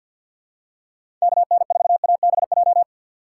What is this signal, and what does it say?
Morse code, heard as a keyed tone, about 51 wpm, 700 Hz. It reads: KN4ABJ
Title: USA Call Signs - 6 Characters - 4x Character Spacing - 50wpm